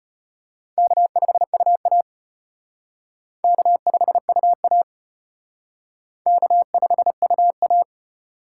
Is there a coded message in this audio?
Yes: Morse code from 0.8 to 7.8 s, heard as a keyed tone, about 32 wpm, 700 Hz. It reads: K5UA K5UA K5UA